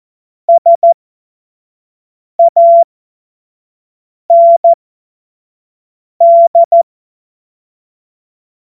Morse code, heard as a keyed tone, about 14 wpm, spaced as for 5 wpm, 680 Hz. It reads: SAND
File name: Morsekurs / Witze / Witz 02. August 2022 25 F70